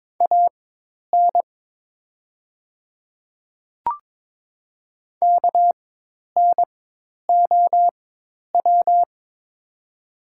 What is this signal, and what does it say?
Morse code, heard as a keyed tone, about 23 wpm, spaced as for 10 wpm, 700 Hz. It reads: AN KNOW
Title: 4x Character Spacing - Top 100 Words - 22wpm